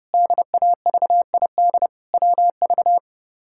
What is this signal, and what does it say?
Morse code, heard as a keyed tone, about 30 wpm, 700 Hz. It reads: DAVID WV
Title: K1USN SST QSO Practice - 30wpm